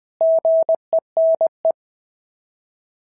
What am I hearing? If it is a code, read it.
Morse code, heard as a keyed tone, about 20 wpm, 690 Hz. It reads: GENE
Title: SOTA QSO Practice 20wpm